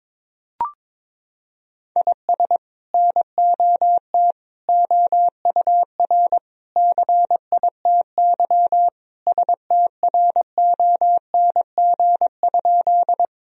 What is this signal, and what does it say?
Morse code, heard as a keyed tone, about 22 wpm, 700 Hz. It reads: IS NOT OUR CITY STRONG?
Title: Sentences from Top 1000 Words - Review Part 3 - 22wpm